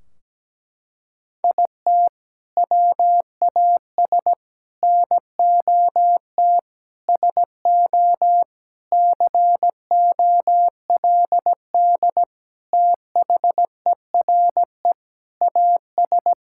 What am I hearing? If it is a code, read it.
Morse code, heard as a keyed tone, about 17 wpm, 700 Hz. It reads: IT WAS NOT SO COLD THERE AS
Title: Sentences from Top 1600 Words 17wpm